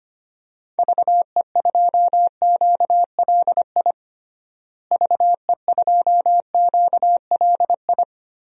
Morse code, heard as a keyed tone, about 25 wpm, 700 Hz. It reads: VE2QLI VE2QLI
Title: DXCC-100 Call Signs - 6 Characters - Repeated 3x - 25wpm